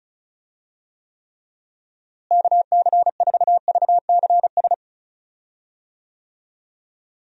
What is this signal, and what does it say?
Morse code, heard as a keyed tone, about 36 wpm, 700 Hz. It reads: KC4VCS
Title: USA Call Signs - 6 Characters - Repeated 2x - 35wpm